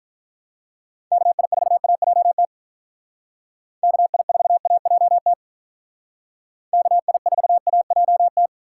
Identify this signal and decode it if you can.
Morse code, heard as a keyed tone, about 45 wpm, 700 Hz. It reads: KI4AJT KI4AJT KI4AJT